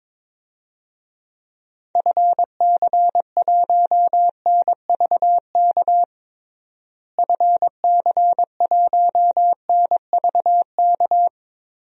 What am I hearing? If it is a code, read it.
Morse code, heard as a keyed tone, about 21 wpm, 700 Hz. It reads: FC1NVK FC1NVK
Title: DXCC-100 Call Signs - 6 Characters - Repeated 3x - 22wpm